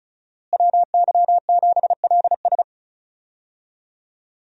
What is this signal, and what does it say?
Morse code, heard as a keyed tone, about 34 wpm, 700 Hz. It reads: WY7LS
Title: USA Call Signs - 5 Characters - Repeated 2x - 35wpm